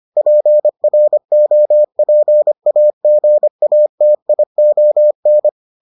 Morse code, heard as a keyed tone, about 25 wpm, 600 Hz. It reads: PROPAGATION